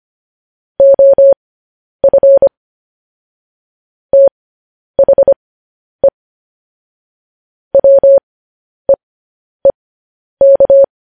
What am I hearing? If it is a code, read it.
Morse code, heard as a keyed tone, about 25 wpm, spaced as for 10 wpm, 550 Hz. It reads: OF THE WEEK
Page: news headlines in morse code at 10 wpm